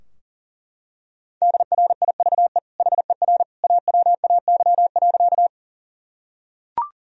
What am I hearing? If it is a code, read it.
Morse code, heard as a keyed tone, about 40 wpm, 700 Hz. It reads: DRIVE HER AWAY.
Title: Sentences from Top 1000 Words - Review Part 2 - 40wpm